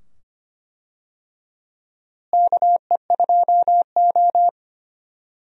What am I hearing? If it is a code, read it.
Morse code, heard as a keyed tone, about 25 wpm, 700 Hz. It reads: KE2O
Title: USA Call Signs - 4 Characters - Repeated 2x - 25wpm